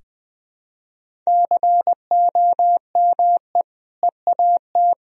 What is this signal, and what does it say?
Morse code, heard as a keyed tone, about 21 wpm, 700 Hz. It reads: COME EAT
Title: Sentences from Top 300 Words and Abbreviations 20wpm